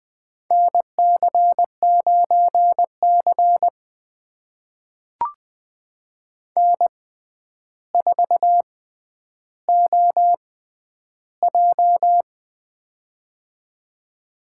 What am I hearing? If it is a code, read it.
Morse code, heard as a keyed tone, about 20 wpm, spaced as for 10 wpm, 700 Hz. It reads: NC9C N4OJ
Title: USA Call Signs - 4 Characters - 6x Character Spacing - 20wpm